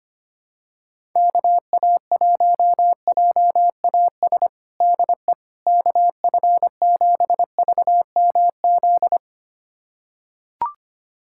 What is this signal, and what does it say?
Morse code, heard as a keyed tone, about 25 wpm, 700 Hz. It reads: KA1JAS DE KF7VMZ
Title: USA Callsign DE Callsign - 6 Characters - 25wpm